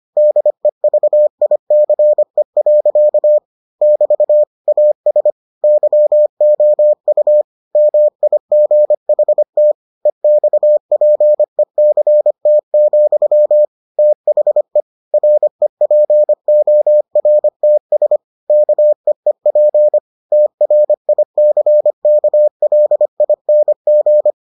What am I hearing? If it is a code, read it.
Morse code, heard as a keyed tone, about 25 wpm, 600 Hz. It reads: DEVICE. <BT> AS YOU MIGHT EXPECT, THE REPORTS KEEP TRICKLING